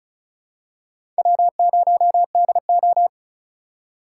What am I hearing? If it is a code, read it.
Morse code, heard as a keyed tone, about 36 wpm, 700 Hz. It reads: W0DO
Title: USA Call Signs - 4 Characters - 3x Character Spacing - 35wpm